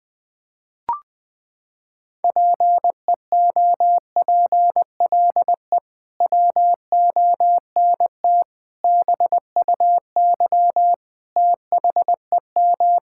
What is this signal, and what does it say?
Morse code, heard as a keyed tone, about 21 wpm, 700 Hz. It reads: PEOPLE WONT BUY THEM
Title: Sentences from Top 1000 Words - Review Part 6 - 20wpm